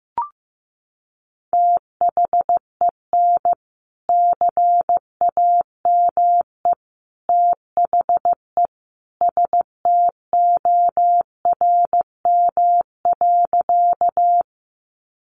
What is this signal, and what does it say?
Morse code, heard as a keyed tone, about 15 wpm, 700 Hz. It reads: THEN CAME THE STORM.
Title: Sentences from Top 1500 Words 15wpm